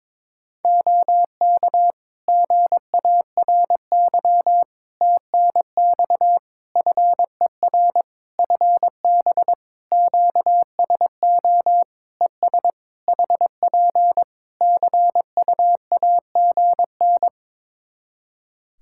Morse code, heard as a keyed tone, about 22 wpm, 700 Hz. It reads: OK GARY TNX FER FB QSO ES HP CUAGN
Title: Practice QSO - Exchange 4 - Ending - 22wpm